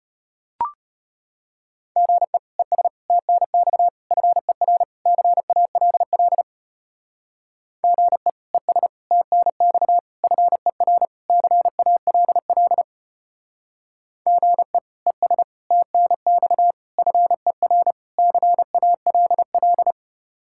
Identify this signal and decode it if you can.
Morse code, heard as a keyed tone, about 30 wpm, 700 Hz. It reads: GE ES TNX FER CALL GE ES TNX FER CALL GE ES TNX FER CALL
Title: Speed-Racing Common QSO Phrases - 25wpm